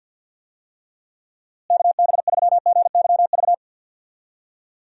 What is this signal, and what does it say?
Morse code, heard as a keyed tone, about 52 wpm, 700 Hz. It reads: KB3GYV
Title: USA Call Signs - 6 Characters - Repeated 2x - 50wpm